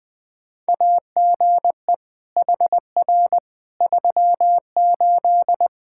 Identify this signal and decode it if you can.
Morse code, heard as a keyed tone, about 21 wpm, 700 Hz. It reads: AGE HR 38
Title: Practice QSO - Exchange 3 - More Info - 20wpm